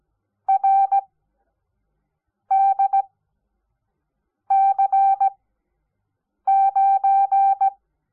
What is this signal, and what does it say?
Morse code, heard as a keyed tone, about 17 wpm, 780 Hz. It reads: R D C 9
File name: Onlinelehrgang / morsen / DJ4UF Morsen05